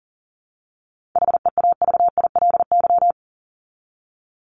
Morse code, heard as a keyed tone, about 37 wpm, 700 Hz. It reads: HEAVILY
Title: LY Suffix - Top 100 Words with Base Words 40wpm